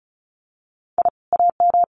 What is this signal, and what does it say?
Morse code, heard as a keyed tone, about 36 wpm, 700 Hz. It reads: I AM
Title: Common QSO Phrases Encore 35wpm